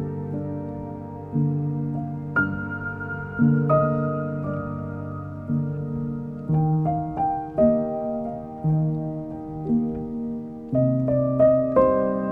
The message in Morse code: none